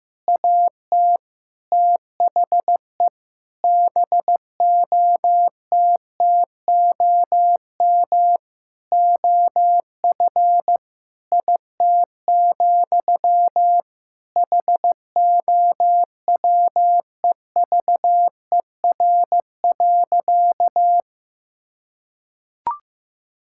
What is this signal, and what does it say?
Morse code, heard as a keyed tone, about 15 wpm, 700 Hz. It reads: AT THE BOTTOM OF IT, HOWEVER.
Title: Sentences from Top 1000 Words - Review Part 6 - 15wpm